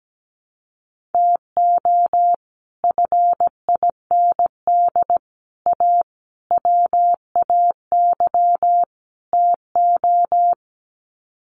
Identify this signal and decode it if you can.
Morse code, heard as a keyed tone, about 17 wpm, 700 Hz. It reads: TO FIND A WAY TO
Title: Sets of 5 Words 17wpm